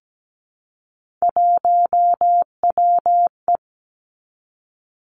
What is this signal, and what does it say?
Morse code, heard as a keyed tone, about 17 wpm, 700 Hz. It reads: 1WE
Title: Three Challenging Letter-Number Combinations 17wpm